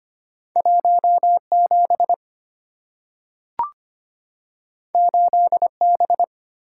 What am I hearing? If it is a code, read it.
Morse code, heard as a keyed tone, about 25 wpm, 700 Hz. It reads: 17 8B
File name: Two Letter-Number Combinations - 25wpm